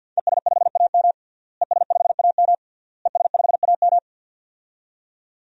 Morse code, heard as a keyed tone, about 47 wpm, 700 Hz. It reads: ES5AM ES5AM ES5AM
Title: DXCC-100 Call Signs - 5 Characters - Repeated 3x - 50wpm